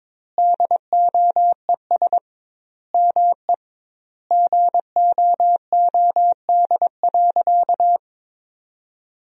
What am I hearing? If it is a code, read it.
Morse code, heard as a keyed tone, about 22 wpm, 700 Hz. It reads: DOES ME GOOD.
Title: 2x Word-Spacing - Sentences from Top 200 Words - 22wpm